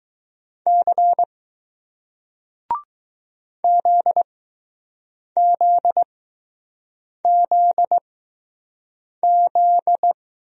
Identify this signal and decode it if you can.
Morse code, heard as a keyed tone, about 20 wpm, spaced as for 6 wpm, 700 Hz. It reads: C ZZZZ